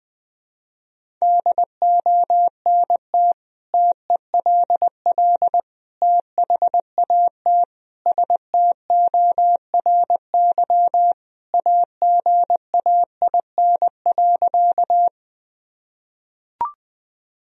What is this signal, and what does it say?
Morse code, heard as a keyed tone, about 20 wpm, 700 Hz. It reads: DONT TELL THAT STORY AGAIN.
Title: Sentences from Top 1000 Words - Review Part 1 - 20wpm